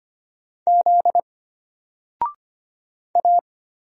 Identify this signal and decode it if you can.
Morse code, heard as a keyed tone, about 25 wpm, 700 Hz. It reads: Z A
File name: Single-Letter-Number-25wpm